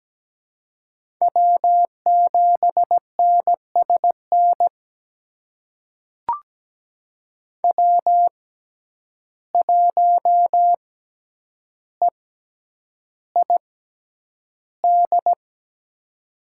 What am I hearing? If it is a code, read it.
Morse code, heard as a keyed tone, about 17 wpm, spaced as for 9 wpm, 700 Hz. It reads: W7NSN W1EID
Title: USA Call Signs - 5 Characters - 6x Character Spacing - 17wpm